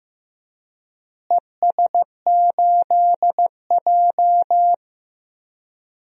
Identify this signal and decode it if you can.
Morse code, heard as a keyed tone, about 15 wpm, 700 Hz. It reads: ES8J
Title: DXCC-100 Call Signs - 4 Characters - Repeated 2x - 15wpm